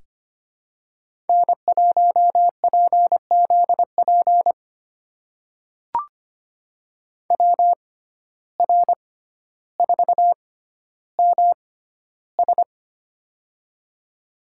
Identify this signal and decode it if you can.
Morse code, heard as a keyed tone, about 25 wpm, spaced as for 13 wpm, 700 Hz. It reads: N1PZP WR4MS
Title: USA Call Signs - 5 Characters - 6x Character Spacing - 25wpm